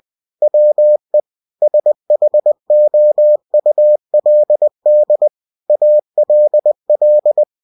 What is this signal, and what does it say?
Morse code, heard as a keyed tone, about 20 wpm, 600 Hz. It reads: WE SHOULD ALL
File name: cw-books / winnie-the-pooh / winnie-20wpm-page-0107